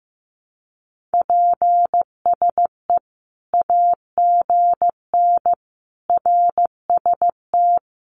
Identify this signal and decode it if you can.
Morse code, heard as a keyed tone, about 15 wpm, 700 Hz. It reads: PSE AGN RST
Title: Common QSO Phrases 15wpm